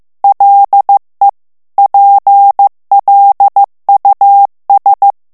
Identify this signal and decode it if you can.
Morse code, heard as a keyed tone, about 15 wpm, 800 Hz. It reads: LE PLUS
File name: morse mp3 / FRANCAIS TEXTE CLAIR 1 VITESSE 14 MOTS MINUTES